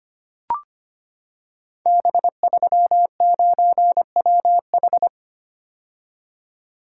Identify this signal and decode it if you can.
Morse code, heard as a keyed tone, about 25 wpm, 700 Hz. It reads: B39WH
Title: Code Groups - 5 Letter:Numbers - 25wpm